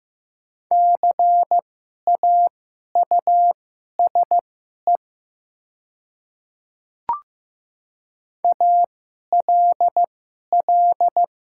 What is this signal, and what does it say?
Morse code, heard as a keyed tone, about 15 wpm, spaced as for 11 wpm, 700 Hz. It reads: CAUSE ALL
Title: 2x Character Spacing - Top 200 Words - 15wpm